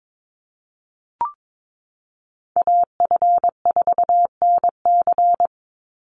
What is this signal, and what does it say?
Morse code, heard as a keyed tone, about 22 wpm, 700 Hz. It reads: AF4NC
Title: Call Signs Repeated 3x - 22wpm